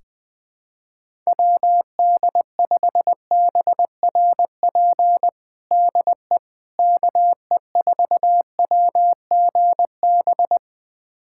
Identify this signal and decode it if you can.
Morse code, heard as a keyed tone, about 20 wpm, 700 Hz. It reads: WD5BRP DE KE4WGB
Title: USA Callsign DE Callsign 20wpm